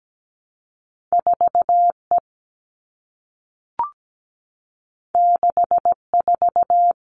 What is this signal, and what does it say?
Morse code, heard as a keyed tone, about 17 wpm, 700 Hz. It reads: 4E 64
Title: All Cut Numbers - 2-Digit 17wpm